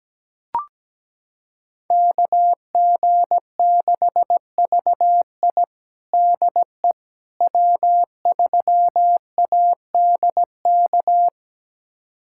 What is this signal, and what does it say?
Morse code, heard as a keyed tone, about 17 wpm, 700 Hz. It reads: KG6VI DE W3ADK
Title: USA Callsign DE Callsign - 5 Characters - 17wpm